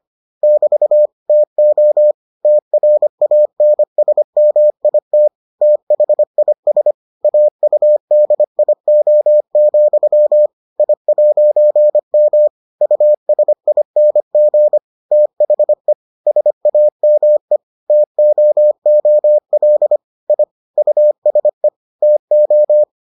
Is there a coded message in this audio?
Yes: Morse code from 0.4 to 22.8 s, heard as a keyed tone, about 25 wpm, 600 Hz. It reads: <BT> TO TRANSMIT THIS AUDIO, I'M USING THE SAME TOOL I USE TO